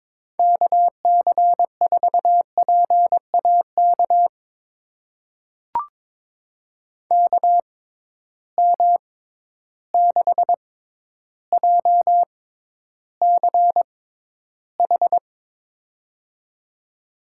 Morse code, heard as a keyed tone, about 22 wpm, spaced as for 11 wpm, 700 Hz. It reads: KC4PAK KM6JCH
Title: USA Call Signs - 6 Characters - 6x Character Spacing - 22wpm